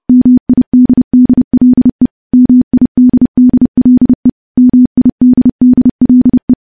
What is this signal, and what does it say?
Morse code, heard as a keyed tone, about 28 wpm, 250 Hz. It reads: MIDDLE MIDDLE MIDDLE
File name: morse / morse-training-30WPM